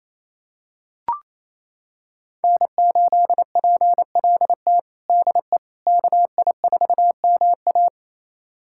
Morse code, heard as a keyed tone, about 28 wpm, 700 Hz. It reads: N8PLT DE KI4MA
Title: USA Callsign DE Callsign - 5 Characters - 28wpm